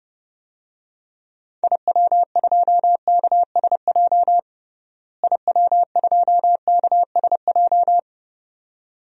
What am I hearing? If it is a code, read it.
Morse code, heard as a keyed tone, about 31 wpm, 700 Hz. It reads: IW2KSJ IW2KSJ
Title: DXCC-100 Call Signs - 6 Characters - Repeated 2x - 30wpm